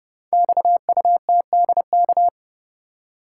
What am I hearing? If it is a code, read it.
Morse code, heard as a keyed tone, about 30 wpm, 700 Hz. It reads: XUTDK
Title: Code Groups - 5 Letters - 30wpm